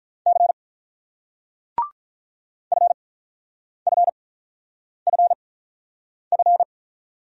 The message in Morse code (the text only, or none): C FFFF